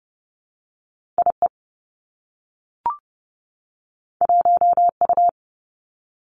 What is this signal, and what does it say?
Morse code, heard as a keyed tone, about 30 wpm, 700 Hz. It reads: I E 1 U
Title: Two Challenging Letter-Number Combinations 30wpm